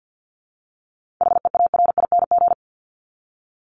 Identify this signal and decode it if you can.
Morse code, heard as a keyed tone, about 50 wpm, 700 Hz. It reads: HEARING
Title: ING Suffix - Top 100 Words with Base Words 50wpm